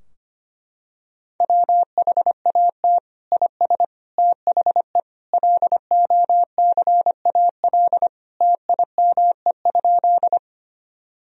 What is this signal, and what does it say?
Morse code, heard as a keyed tone, about 25 wpm, 700 Hz. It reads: WHAT IS THE LOCAL TIME?
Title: Sentences from Top 1000 Words - Review Part 3 - 25wpm